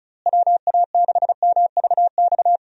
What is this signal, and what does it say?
Morse code, heard as a keyed tone, about 36 wpm, 700 Hz. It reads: WA6MVX
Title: USA Call Signs - 6 Characters - Repeated 2x - 35wpm